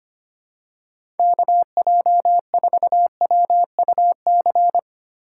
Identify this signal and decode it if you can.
Morse code, heard as a keyed tone, about 25 wpm, 700 Hz. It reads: KJ4WUC
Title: USA Call Signs - 6 Characters - 8x Character Spacing - 25wpm